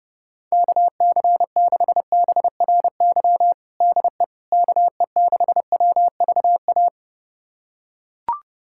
Morse code, heard as a keyed tone, about 29 wpm, 700 Hz. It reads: KC6BRY DE KE6WVA
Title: USA Callsign DE Callsign - 6 Characters - 30wpm